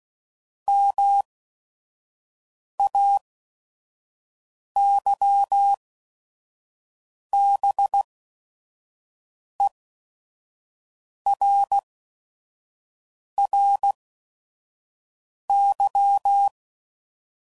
Morse code, heard as a keyed tone, about 16 wpm, spaced as for 5 wpm, 800 Hz. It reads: MAYBERRY
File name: CW Code Groups @5WPM / fromSM's / Cities